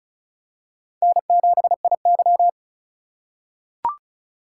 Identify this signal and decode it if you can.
Morse code, heard as a keyed tone, about 36 wpm, 700 Hz. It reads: N7IY
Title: USA Call Signs - 4 Characters - 4x Character Spacing - 35wpm